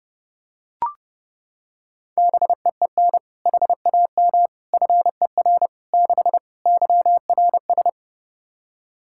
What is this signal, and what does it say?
Morse code, heard as a keyed tone, about 29 wpm, 700 Hz. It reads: BEEN HAM FER 6 YRS
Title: Practice QSO - Exchange 3 - More Info - 30wpm